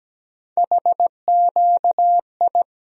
Morse code, heard as a keyed tone, about 17 wpm, 700 Hz. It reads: HQI
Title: USA Call Sign Postfixes - 3 Letters 17wpm